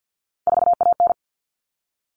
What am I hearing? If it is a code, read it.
Morse code, heard as a keyed tone, about 50 wpm, 700 Hz. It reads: 4AN
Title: Common Cut Numbers - 3-Digit 50wpm